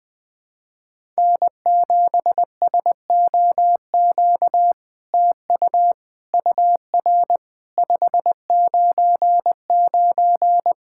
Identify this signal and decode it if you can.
Morse code, heard as a keyed tone, about 20 wpm, 700 Hz. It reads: N7SOQ TU UR 599
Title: SOTA QSO Practice 20wpm